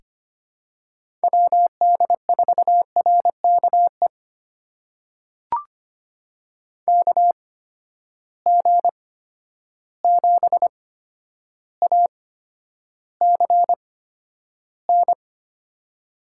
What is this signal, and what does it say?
Morse code, heard as a keyed tone, about 25 wpm, spaced as for 11 wpm, 700 Hz. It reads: WD4RKE KG7ACN
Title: USA Call Signs - 6 Characters - 8x Character Spacing - 25wpm